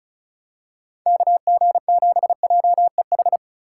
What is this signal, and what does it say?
Morse code, heard as a keyed tone, about 36 wpm, 700 Hz. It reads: KG7JEH
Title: USA Call Signs - 6 Characters - 4x Character Spacing - 35wpm